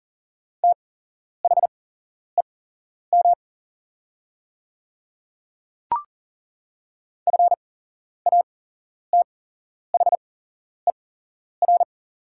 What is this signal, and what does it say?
Morse code, heard as a keyed tone, about 40 wpm, spaced as for 11 wpm, 700 Hz. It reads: THEM FATHER
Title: 8x Character Spacing - Top 200 Words - 40wpm